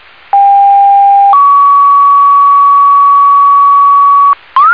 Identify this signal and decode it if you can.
Two-tone page, heard as a one-second tone, then a three-second tone, about 800 then 1130 Hz, with noise. It is heard from 0.3 to 4.3 s.